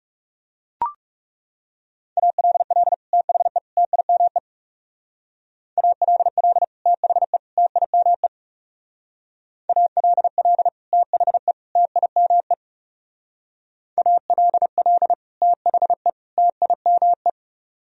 Morse code, heard as a keyed tone, about 37 wpm, 700 Hz. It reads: ALL THE TIME ALL THE TIME ALL THE TIME ALL THE TIME
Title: Speed-Racing - Sets of 3 Words 30wpm